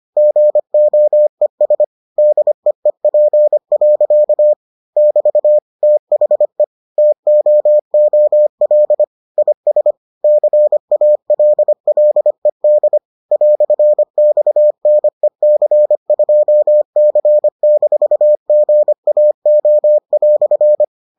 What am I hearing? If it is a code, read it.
Morse code, heard as a keyed tone, about 25 wpm, 600 Hz. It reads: GOES DEEP. <BT> THE TOOL IS CALLED "XNEC2C-GAO"